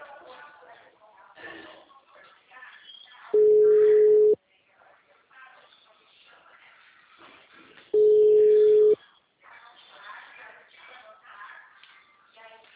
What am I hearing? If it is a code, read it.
Ringback tone (Europe, 425 Hz), with speech.